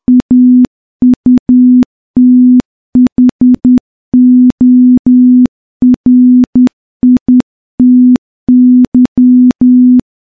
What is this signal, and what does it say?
Morse code, heard as a keyed tone, about 10 wpm, 250 Hz. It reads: AUTHORITY
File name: morse-training-10WPM